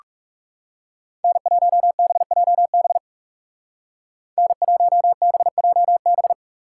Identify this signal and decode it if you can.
Morse code, heard as a keyed tone, about 35 wpm, 700 Hz. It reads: N1BJB N1BJB
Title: Speed Racing - USA 1x3 Call Signs 30wpm